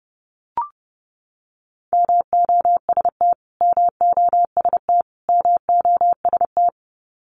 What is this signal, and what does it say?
Morse code, heard as a keyed tone, about 30 wpm, 700 Hz. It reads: MOST MOST MOST
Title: Top 300 Words Repeated 30wpm